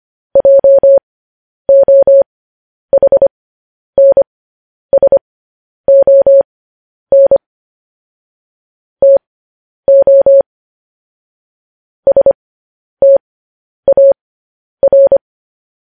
Morse code, heard as a keyed tone, about 25 wpm, spaced as for 10 wpm, 550 Hz. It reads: JOHNSON TO STAR